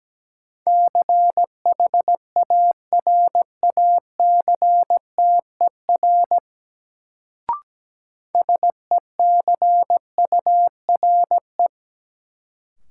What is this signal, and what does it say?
Morse code, heard as a keyed tone, about 17 wpm, 700 Hz. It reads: CHARACTER SECURE
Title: Sentences from Top 1200 Words - New Word Drill - 17wpm